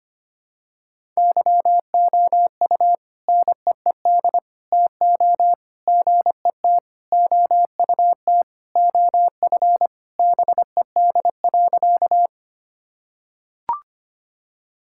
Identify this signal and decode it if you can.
Morse code, heard as a keyed tone, about 25 wpm, 700 Hz. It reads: YOU NEED TO GET OUT OF BED.
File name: practice / 25wpm / Sentences with 3-Word Phrases - 25wpm